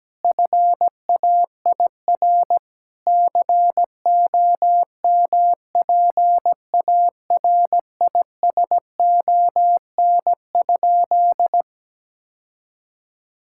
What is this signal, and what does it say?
Morse code, heard as a keyed tone, about 17 wpm, 700 Hz. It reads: FAIR COMPARISON?
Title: Sentences from Top 1600 Words 17wpm